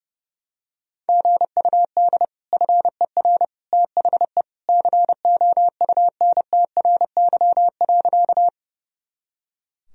Morse code, heard as a keyed tone, about 31 wpm, 700 Hz. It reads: GUD FER THE COUNTRY.
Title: Sentences from Top 300 Words and Abbreviations 30wpm